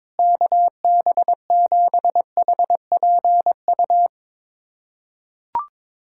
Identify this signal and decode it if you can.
Morse code, heard as a keyed tone, about 21 wpm, 700 Hz. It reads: KB7HPU
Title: USA Callsign DE Callsign - 6 Characters - 22wpm